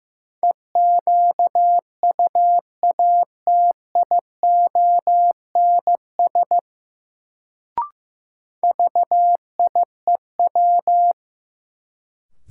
Morse code, heard as a keyed tone, about 15 wpm, 700 Hz. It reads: EQUATIONS VIEW
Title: Sets of 5 Words - New Word Drill - 15wpm